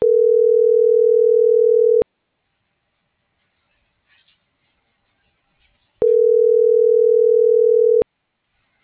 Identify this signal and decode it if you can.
Ringback tone (North America), 440 + 480 Hz.